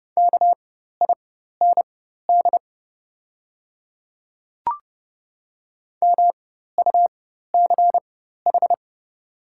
Morse code, heard as a keyed tone, about 29 wpm, spaced as for 14 wpm, 700 Hz. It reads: KIND MUCH
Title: 4x Character Spacing - Top 200 Words - 30wpm